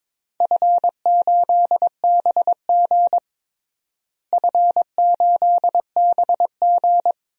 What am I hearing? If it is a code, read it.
Morse code, heard as a keyed tone, about 22 wpm, 700 Hz. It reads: F8BG F8BG
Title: DXCC-100 Call Signs - 4 Characters - Repeated 2x - 22wpm